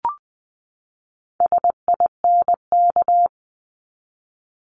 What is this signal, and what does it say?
Morse code, heard as a keyed tone, about 20 wpm, 700 Hz. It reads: SINK
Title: Top 1751-2000 Words 20wpm